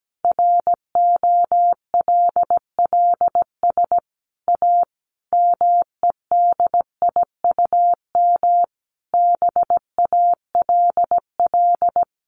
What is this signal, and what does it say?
Morse code, heard as a keyed tone, about 17 wpm, 700 Hz. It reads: ROLLS A MEDIUM BALL
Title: Easy Phrases 17wpm